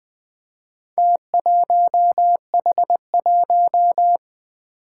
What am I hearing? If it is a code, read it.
Morse code, heard as a keyed tone, about 20 wpm, 700 Hz. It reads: T1H1
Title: Four Challenging Letter-Number Combinations 20wpm